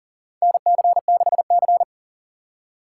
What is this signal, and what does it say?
Morse code, heard as a keyed tone, about 38 wpm, 700 Hz. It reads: NC6C
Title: USA Call Signs - 4 Characters - 2x Character Spacing - 40wpm